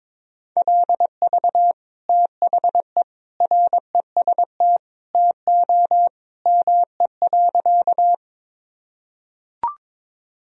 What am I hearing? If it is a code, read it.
Morse code, heard as a keyed tone, about 21 wpm, 700 Hz. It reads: LV THE REST TO ME.
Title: Sentences from Top 500 Words and Abbreviations 22wpm